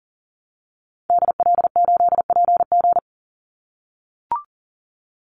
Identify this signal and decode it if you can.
Morse code, heard as a keyed tone, about 40 wpm, 700 Hz. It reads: DL8PG
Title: Call Signs - 40wpm